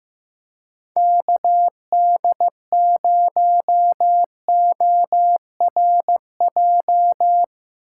Morse code, heard as a keyed tone, about 15 wpm, 700 Hz. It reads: KD0ORJ
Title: USA Call Signs - 6 Characters - 2x Character Spacing - 15wpm